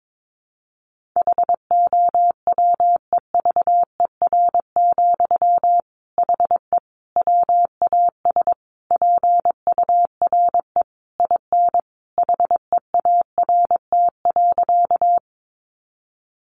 Morse code, heard as a keyed tone, about 22 wpm, 700 Hz. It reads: HOWEVER, HE WAS PURE IN HEART.
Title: Sentences from Top 1500 Words 22wpm